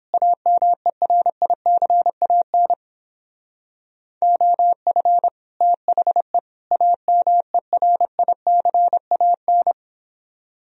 Morse code, heard as a keyed tone, about 29 wpm, 700 Hz. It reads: AMERICAN OF THE AMERICAN